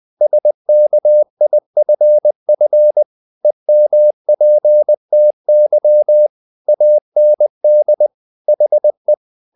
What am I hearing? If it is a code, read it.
Morse code, heard as a keyed tone, about 20 wpm, 600 Hz. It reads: SKIFF EMPTY AND HE
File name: the old man and the sea / the old man and the sea part 1 20wpm